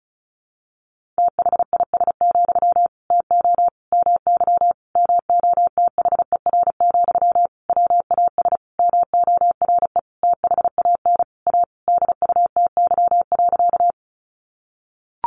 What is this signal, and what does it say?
Morse code, heard as a keyed tone, about 34 wpm, 700 Hz. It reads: THIS, TO MY MOTHER, WAS MORE THAN A DUTY.
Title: Sentences from Top 1500 Words 35wpm